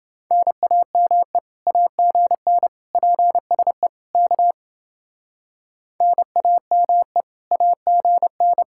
Morse code, heard as a keyed tone, about 29 wpm, 700 Hz. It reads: NAME AGN PSE K NAME AGN